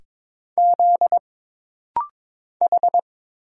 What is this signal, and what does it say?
Morse code, heard as a keyed tone, about 22 wpm, 700 Hz. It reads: Z H
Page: This rapid-fire timing is geared to push your ICR (Instant Character Recognition) skills.